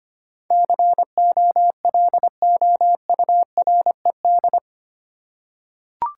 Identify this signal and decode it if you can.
Morse code, heard as a keyed tone, about 25 wpm, 700 Hz. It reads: COLOURED
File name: 8 Letter Words - First 3 Letters Given - 25wpm